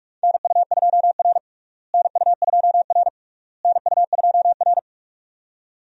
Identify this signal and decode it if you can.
Morse code, heard as a keyed tone, about 41 wpm, 700 Hz. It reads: NU2R NU2R NU2R
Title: USA Call Signs - 4 Characters - Repeated 3x - 45wpm